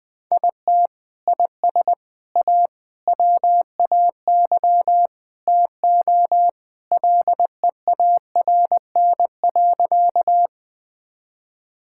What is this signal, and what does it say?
Morse code, heard as a keyed tone, about 20 wpm, 700 Hz. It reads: IT IS A WAY TO LEARN.
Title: Sentences with 3-Word Phrases - Spoken Beginning - 20wpm